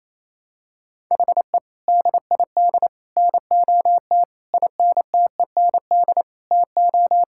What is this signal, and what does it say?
Morse code, heard as a keyed tone, about 28 wpm, 700 Hz. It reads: HE DID NOT INTEND TO
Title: Sentences from Top 1600 Words 28wpm